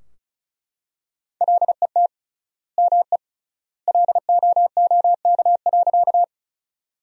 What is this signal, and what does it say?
Morse code, heard as a keyed tone, about 36 wpm, 700 Hz. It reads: LET ME LOOK.
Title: 3x Word-Spacing - Sentences from Top 300 Words - 35wpm